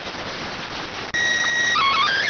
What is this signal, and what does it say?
SSTV (slow-scan television), PD 290.